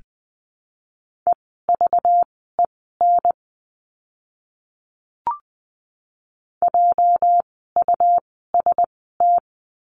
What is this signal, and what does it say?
Morse code, heard as a keyed tone, about 20 wpm, spaced as for 14 wpm, 700 Hz. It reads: EVEN JUST
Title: 237 Lesson 38f - Letter q - Words 20wpm